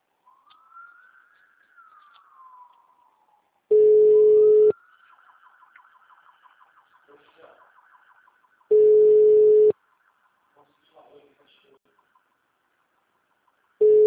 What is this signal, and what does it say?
Ringback tone (Europe, 425 Hz), with speech.
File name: out- -2051- - - .34589